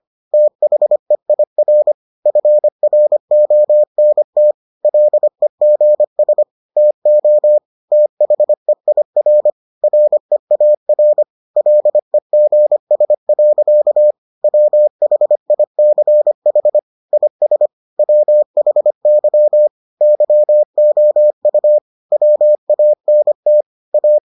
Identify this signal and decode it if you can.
Morse code, heard as a keyed tone, about 25 wpm, 600 Hz. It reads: THEIR FRONT LEGS TO THEIR REAR LEGS. WHICH IS WHY YOU WANT A